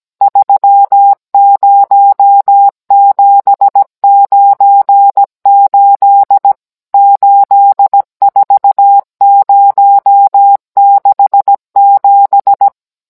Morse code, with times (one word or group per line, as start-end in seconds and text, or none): 0.2-6.5 30798
6.9-12.7 84067